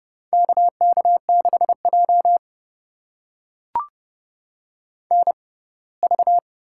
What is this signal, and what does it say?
Morse code, heard as a keyed tone, about 32 wpm, spaced as for 21 wpm, 700 Hz. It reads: KK6J NV
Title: USA Call Signs - 6x Character Spacing - 30wpm